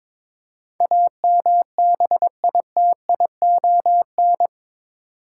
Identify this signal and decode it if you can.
Morse code, heard as a keyed tone, about 22 wpm, 700 Hz. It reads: AMBITION
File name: practice / 22wpm / Eight Letter Words - 22wpm